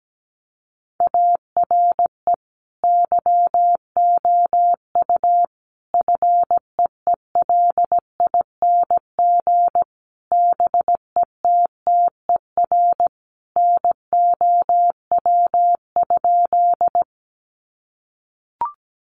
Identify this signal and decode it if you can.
Morse code, heard as a keyed tone, about 17 wpm, 700 Hz. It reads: ARE YOU FEELING BETTER NOW?
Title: Sentences from Top 1200 Words 17wpm